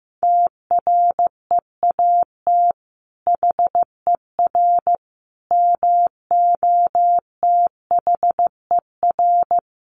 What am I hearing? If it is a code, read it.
Morse code, heard as a keyed tone, about 15 wpm, 700 Hz. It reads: TREAT HER MOTHER
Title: Sentences from Top 1300 Words 15wpm